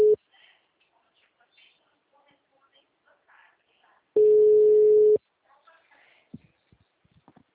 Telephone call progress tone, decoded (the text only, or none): ringback tone